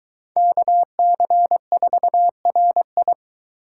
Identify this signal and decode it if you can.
Morse code, heard as a keyed tone, about 23 wpm, 700 Hz. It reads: KC4RI